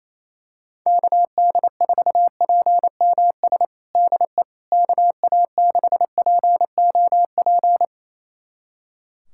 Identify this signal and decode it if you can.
Morse code, heard as a keyed tone, about 28 wpm, 700 Hz. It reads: KD4PMS DE KA6POP
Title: USA Callsign DE Callsign 28wpm